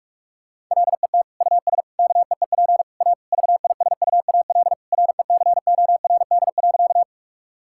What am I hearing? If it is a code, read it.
Morse code, heard as a keyed tone, about 46 wpm, 700 Hz. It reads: LET US KEEP A VISUAL RECORD.
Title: Sentences from Top 1000 Words - Review Part 6 - 45wpm